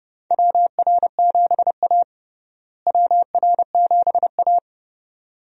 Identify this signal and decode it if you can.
Morse code, heard as a keyed tone, about 30 wpm, 700 Hz. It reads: WR7A WR7A
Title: USA Call Signs - 4 Characters - Repeated 2x - 30wpm